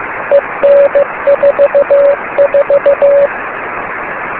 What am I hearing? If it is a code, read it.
Morse code, heard as a keyed tone, about 15 wpm, 560 Hz, with noise. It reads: EN44
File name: beacons / k9mu